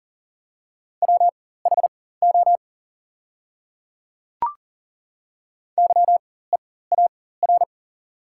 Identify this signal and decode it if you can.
Morse code, heard as a keyed tone, about 40 wpm, spaced as for 19 wpm, 700 Hz. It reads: WHO YEAR
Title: 4x Character Spacing - Top 200 Words - 40wpm